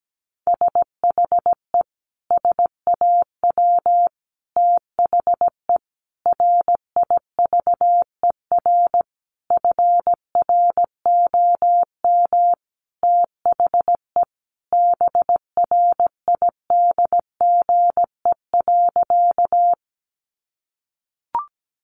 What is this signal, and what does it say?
Morse code, heard as a keyed tone, about 17 wpm, 700 Hz. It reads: SHE SAW THE RIVER FROM THE BRIDGE.
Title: Sentences from Top 1200 Words 17wpm